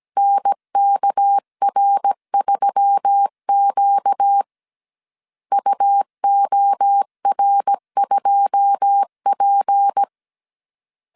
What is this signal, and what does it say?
Morse code, heard as a keyed tone, about 17 wpm, 800 Hz. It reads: NKR3Q UOR2P
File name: CW / DET / 1 9 1500C S A 16wpm 2828